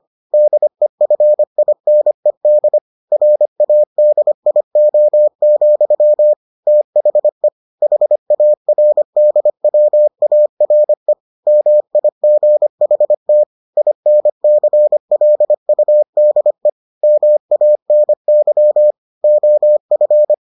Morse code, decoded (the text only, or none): DEFINED RADIO, THE HARDWARE MIGHT INCLUDE MANY OF